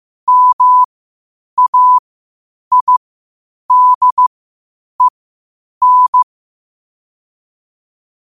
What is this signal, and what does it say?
Morse code, heard as a keyed tone, about 15 wpm, spaced as for 8 wpm, 1000 Hz. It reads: MAIDEN
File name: qotdfiles / 8wpm